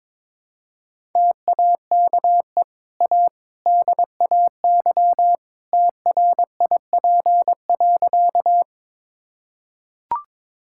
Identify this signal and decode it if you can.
Morse code, heard as a keyed tone, about 21 wpm, 690 Hz. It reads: TAKE A DAY TRIP.
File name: practice / 22wpm / Easy Sentences with Hints - 22wpm